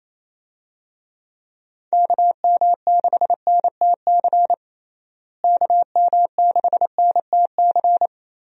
Morse code, heard as a keyed tone, about 28 wpm, 700 Hz. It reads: KM6NTC KM6NTC
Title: USA Call Signs - 6 Characters - Repeated 3x - 28wpm